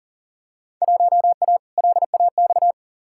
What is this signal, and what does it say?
Morse code, heard as a keyed tone, about 40 wpm, 700 Hz. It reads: 1A LAX
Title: Field Day Exchanges - 40wpm